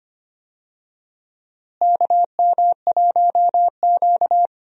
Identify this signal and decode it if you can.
Morse code, heard as a keyed tone, about 25 wpm, 700 Hz. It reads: KM1Q
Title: USA Call Signs - 4 Characters - Repeated 3x - 25wpm